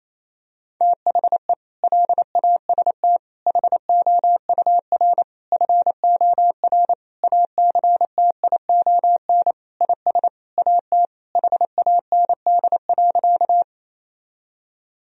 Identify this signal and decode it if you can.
Morse code, heard as a keyed tone, about 28 wpm, 700 Hz. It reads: THE LAST HOUR FOR ACTION IS AT HAND.
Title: Sentences from Top 1000 Words - Review Part 4 - 28wpm